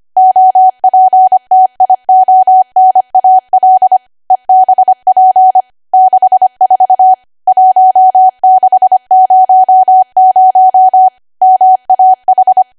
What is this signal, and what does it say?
Morse code, heard as a keyed tone, about 25 wpm, 750 Hz. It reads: OPTIONAL EBP 64 1600 MAH
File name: Archive / 25 WPM / 25WPM